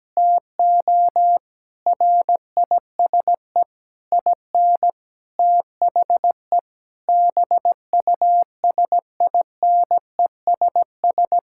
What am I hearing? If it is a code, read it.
Morse code, heard as a keyed tone, about 17 wpm, 700 Hz. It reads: TO RISE IN THE BUSINESS
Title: Sentences from Top 1000 Words - Review Part 6 - 17wpm